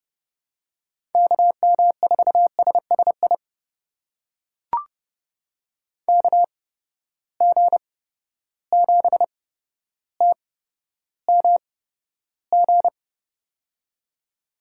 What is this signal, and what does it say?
Morse code, heard as a keyed tone, about 30 wpm, spaced as for 13 wpm, 700 Hz. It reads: KM4SSI KG7TMG
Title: USA Call Signs - 6 Characters - 8x Character Spacing - 30wpm